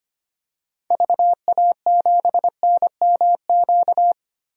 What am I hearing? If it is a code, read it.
Morse code, heard as a keyed tone, about 25 wpm, 700 Hz. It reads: VA7NMQ
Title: DXCC-100 Call Signs - 6 Characters - 25wpm